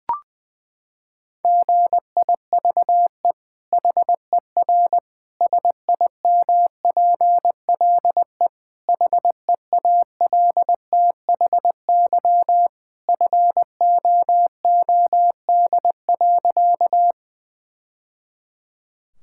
Morse code, heard as a keyed tone, about 21 wpm, 700 Hz. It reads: GIVE HER SIMPLE HEALTHY FOOD.
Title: Sentences from Top 1000 Words - Review Part 5 - 20wpm